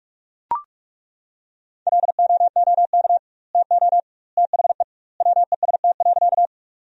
Morse code, heard as a keyed tone, about 41 wpm, 700 Hz. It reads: LOOK TO THE WEST.
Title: 2x Word-Spacing - Sentences from Top 400 Words - 45wpm